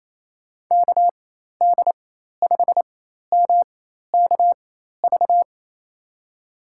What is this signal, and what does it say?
Morse code, heard as a keyed tone, about 28 wpm, spaced as for 13 wpm, 700 Hz. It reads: KD5MKV
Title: USA Call Signs - 6 Characters - 4x Character Spacing - 28wpm